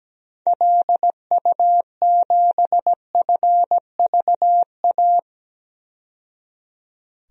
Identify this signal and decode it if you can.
Morse code, heard as a keyed tone, about 17 wpm, 700 Hz. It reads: LU7FVA
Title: DXCC-100 Call Signs - 6 Characters - 17wpm